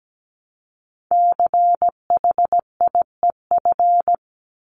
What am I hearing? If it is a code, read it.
Morse code, heard as a keyed tone, about 17 wpm, 700 Hz. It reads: CHIEF
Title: LY Suffix - Top 100 Words with Base Words 17wpm